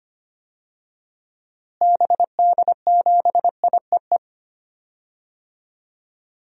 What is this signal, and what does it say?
Morse code, heard as a keyed tone, about 25 wpm, 700 Hz. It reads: BD7IEE
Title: DXCC-100 Call Signs - 6 Characters - Repeated 2x - 25wpm